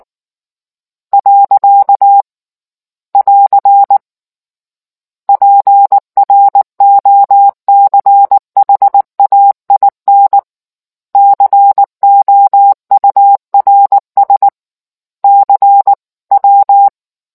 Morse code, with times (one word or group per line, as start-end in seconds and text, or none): 1.1-2.2 .
3.1-4.0 <AR>
5.3-10.4 PROCHAIN
11.2-14.5 COURS
15.2-16.9 CW